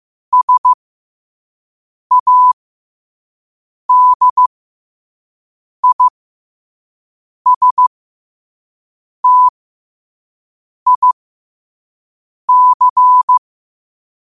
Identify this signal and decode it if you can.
Morse code, heard as a keyed tone, about 15 wpm, spaced as for 5 wpm, 1000 Hz. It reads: SADISTIC
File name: qotdfiles / 5wpm